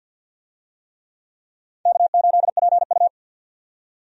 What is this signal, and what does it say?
Morse code, heard as a keyed tone, about 52 wpm, 700 Hz. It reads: K8PU
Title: USA Call Signs - 4 Characters - 50wpm